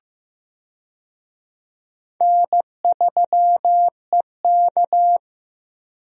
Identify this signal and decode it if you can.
Morse code, heard as a keyed tone, about 15 wpm, 700 Hz. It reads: N3EK
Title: USA Call Signs - 4 Characters - Repeated 2x - 15wpm